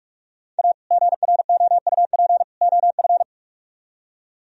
Morse code, heard as a keyed tone, about 43 wpm, 700 Hz. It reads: A GROUP OF